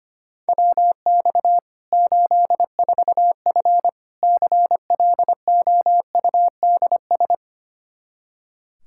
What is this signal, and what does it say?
Morse code, heard as a keyed tone, about 25 wpm, 700 Hz. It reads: WX 84F CLOUDS
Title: Practice QSO - Exchange 2 - Info - 25wpm